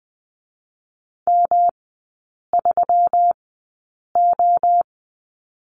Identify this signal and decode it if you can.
Morse code, heard as a keyed tone, about 20 wpm, 700 Hz. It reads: M 3 O
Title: 121 Lesson 20e - Number 3 - Three Characters 20wpm